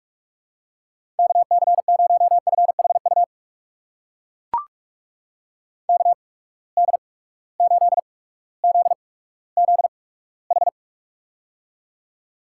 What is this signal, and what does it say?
Morse code, heard as a keyed tone, about 44 wpm, spaced as for 19 wpm, 700 Hz. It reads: KC0FHU KD8ZZH
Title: USA Call Signs - 6 Characters - 8x Character Spacing - 45wpm